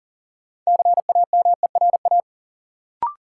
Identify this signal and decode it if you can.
Morse code, heard as a keyed tone, about 40 wpm, 700 Hz. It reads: CAMERA
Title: Sentences from Top 1000 Words - Review Part 5 - New Word Drill - 40wpm